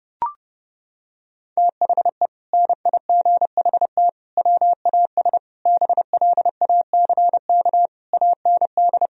Morse code, heard as a keyed tone, about 32 wpm, 700 Hz. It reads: THE NIGHT WAS BLACK AND
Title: Sentences from Top 1000 Words - Review Part 2 - 30wpm